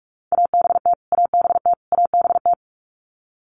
Morse code, heard as a keyed tone, about 44 wpm, 700 Hz. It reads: ABT ABT ABT
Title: Abbreviations Repeated 3x - 45wpm